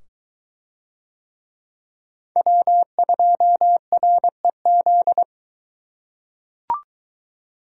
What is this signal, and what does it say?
Morse code, heard as a keyed tone, about 23 wpm, 700 Hz. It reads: W2REZ